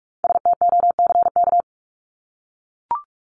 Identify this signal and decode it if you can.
Morse code, heard as a keyed tone, about 44 wpm, 700 Hz. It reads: STOCK
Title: Top 501-750 Words 45wpm